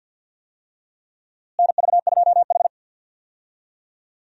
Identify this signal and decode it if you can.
Morse code, heard as a keyed tone, about 46 wpm, 700 Hz. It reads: NV2H
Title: USA Call Signs - 4 Characters - Repeated 2x - 50wpm